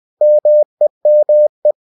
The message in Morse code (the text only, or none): MEME